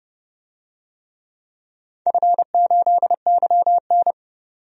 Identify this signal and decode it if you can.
Morse code, heard as a keyed tone, about 31 wpm, 700 Hz. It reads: F8YN
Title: DXCC-100 Call Signs - 4 Characters - 30wpm